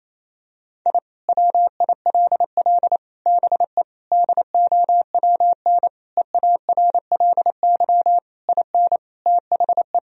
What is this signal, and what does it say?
Morse code, heard as a keyed tone, about 28 wpm, 700 Hz. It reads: I WILL BE DOWN EARLY IN THE
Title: Sentences from Top 1000 Words - Review Part 2 - 28wpm